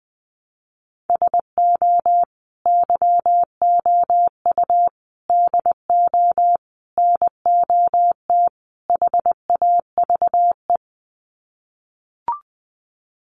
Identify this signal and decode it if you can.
Morse code, heard as a keyed tone, about 20 wpm, 700 Hz. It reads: SO YOU DO NOT HAVE
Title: Sets of 5 Words Encore 20wpm